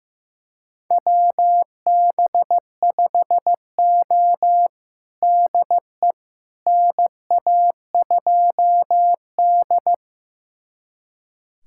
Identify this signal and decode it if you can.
Morse code, heard as a keyed tone, about 15 wpm, 700 Hz. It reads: WB5O DE NA2D
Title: USA Callsign DE Callsign - 4 Characters - 15wpm